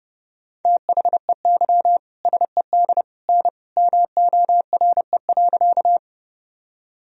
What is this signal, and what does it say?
Morse code, heard as a keyed tone, about 30 wpm, 700 Hz. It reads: THEY SED N MORE.
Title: Sentences from Top 100 Words and Abbreviations 30wpm